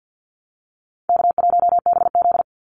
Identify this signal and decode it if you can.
Morse code, heard as a keyed tone, about 50 wpm, 700 Hz. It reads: K1BZ
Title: Easy Call Signs - 50wpm